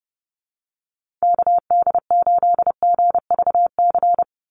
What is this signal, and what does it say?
Morse code, heard as a keyed tone, about 29 wpm, 700 Hz. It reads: KD8GVC
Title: USA Call Signs 30wpm